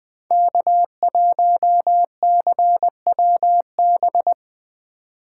Morse code, heard as a keyed tone, about 20 wpm, 700 Hz. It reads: K1CWB
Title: 2023 POTA Hunter Call Signs - 20wpm